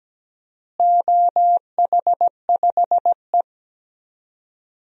Morse code, heard as a keyed tone, about 17 wpm, 700 Hz. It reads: OH5E
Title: DXCC-100 Call Signs - 4 Characters - Repeated 2x - 17wpm